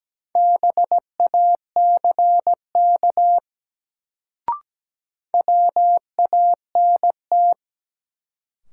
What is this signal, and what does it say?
Morse code, heard as a keyed tone, about 17 wpm, 700 Hz. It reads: BACK WANT
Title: Sentences from Top 200 Words - New Word Drill - 17wpm